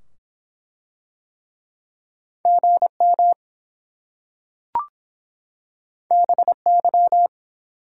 Morse code, heard as a keyed tone, about 24 wpm, 700 Hz. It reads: GM BY